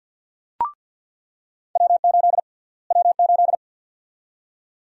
Morse code, heard as a keyed tone, about 46 wpm, 700 Hz. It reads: W8 W8
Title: USA Call Sign Prefixes - Repeated 2x - 50wpm